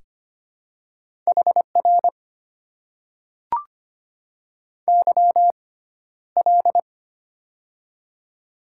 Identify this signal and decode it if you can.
Morse code, heard as a keyed tone, about 25 wpm, spaced as for 13 wpm, 700 Hz. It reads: HR YL
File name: Top 90 QSO Elements - 6x Character Spacing - 25wpm